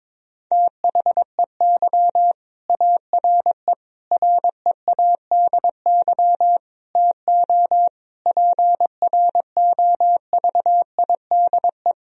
Morse code, heard as a keyed tone, about 22 wpm, 700 Hz. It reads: THEY ARE READY TO PROVIDE
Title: Sentences from Top 1000 Words - Review Part 3 - 22wpm